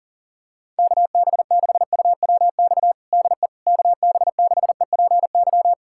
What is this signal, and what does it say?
Morse code, heard as a keyed tone, about 40 wpm, 700 Hz. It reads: KB6UWX DE KB6EPY
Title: USA Callsign DE Callsign 40wpm